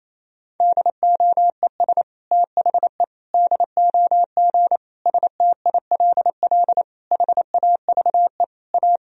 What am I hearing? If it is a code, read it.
Morse code, heard as a keyed tone, about 27 wpm, 700 Hz. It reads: DOES THE DOG STILL HAVE A
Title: Sentences from Top 1000 Words - Review Part 3 - 28wpm